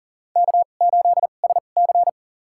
Morse code, heard as a keyed tone, about 40 wpm, spaced as for 29 wpm, 700 Hz. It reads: K8SC
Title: USA Call Signs - 4 Characters - 2x Character Spacing - 40wpm